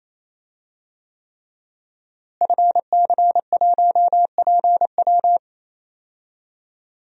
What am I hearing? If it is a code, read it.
Morse code, heard as a keyed tone, about 27 wpm, 700 Hz. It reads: FC1PW
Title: DXCC-100 Call Signs - 5 Characters - Repeated 3x - 28wpm